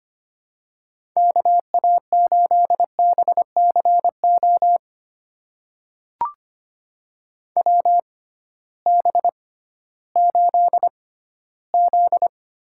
Morse code, heard as a keyed tone, about 25 wpm, spaced as for 14 wpm, 700 Hz. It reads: KA8BCO WB8Z
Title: USA Call Signs - 6 Characters - 6x Character Spacing - 25wpm